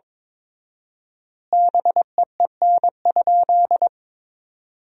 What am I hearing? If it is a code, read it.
Morse code, heard as a keyed tone, about 22 wpm, 700 Hz. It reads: BEEN?
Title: 4x Word-Spacing - Sentences from Top 200 Words - 22wpm